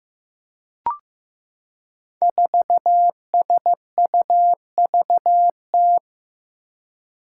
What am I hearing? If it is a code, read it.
Morse code, heard as a keyed tone, about 15 wpm, 700 Hz. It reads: 4SUVT
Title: Five Challenging Letter-Number Combinations 15wpm